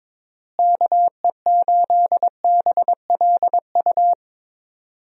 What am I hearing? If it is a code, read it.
Morse code, heard as a keyed tone, about 22 wpm, 700 Hz. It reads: KE8BLU
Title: USA Call Signs - 6 Characters - Repeated 3x - 22wpm